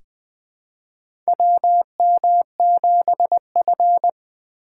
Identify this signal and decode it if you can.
Morse code, heard as a keyed tone, about 20 wpm, 700 Hz. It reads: WM7F
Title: USA Call Signs - 2x Character Spacing - 20wpm